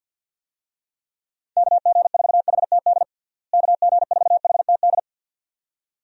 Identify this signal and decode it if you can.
Morse code, heard as a keyed tone, about 52 wpm, 700 Hz. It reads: KG4HTD KG4HTD
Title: USA Call Signs - 6 Characters - Repeated 2x - 50wpm